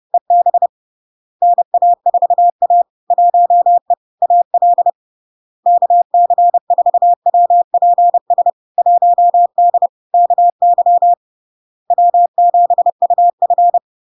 Morse code, heard as a keyed tone, about 30 wpm, 700 Hz. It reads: EB NA4A 1E AL KC4WPS 1D KY W7UF
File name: LARK-FD-2022-Log-in-CW-30wpm-700-Hz